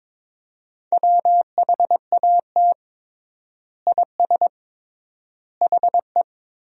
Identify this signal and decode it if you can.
Morse code, heard as a keyed tone, about 22 wpm, 700 Hz. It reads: WHAT IS HE
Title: 3x Word-Spacing - Sentences from Top 100 Words - 22wpm